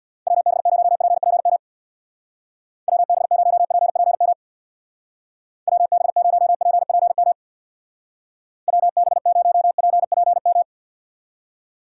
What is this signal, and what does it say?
Morse code, heard as a keyed tone, about 61 wpm, 700 Hz. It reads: WB0PPM WB0PPM WB0PPM WB0PPM
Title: Speed Racing - USA Call Signs 50wpm